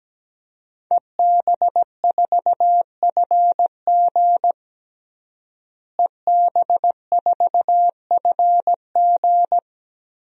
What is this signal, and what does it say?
Morse code, heard as a keyed tone, about 17 wpm, 700 Hz. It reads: EB4FG EB4FG
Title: DXCC-100 Call Signs - 5 Characters - Repeated 3x - 17wpm